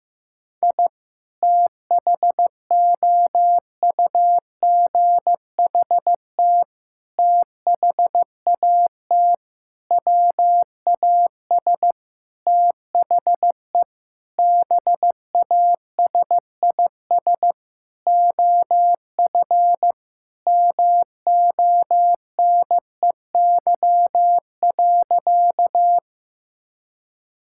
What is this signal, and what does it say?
Morse code, heard as a keyed tone, about 15 wpm, 700 Hz. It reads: I THOUGHT THAT WAS THE BASIS OF MONEY.
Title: Sentences from Top 1000 Words - Review Part 5 - 15wpm